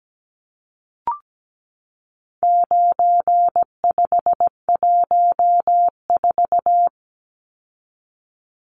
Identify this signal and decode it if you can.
Morse code, heard as a keyed tone, about 17 wpm, 700 Hz. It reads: 9514
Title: Numbers - 4-Digit 17wpm